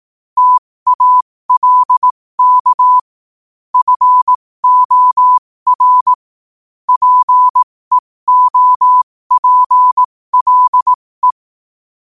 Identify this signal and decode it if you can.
Morse code, heard as a keyed tone, about 18 wpm, spaced as for 15 wpm, 1000 Hz. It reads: TALK FOR PEOPLE
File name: qotdfiles / 15wpm